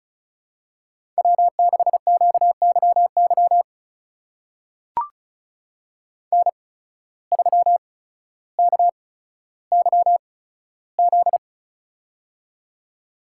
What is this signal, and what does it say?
Morse code, heard as a keyed tone, about 35 wpm, spaced as for 15 wpm, 700 Hz. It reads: W6QYY N3KYZ
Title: USA Call Signs - 5 Characters - 8x Character Spacing - 35wpm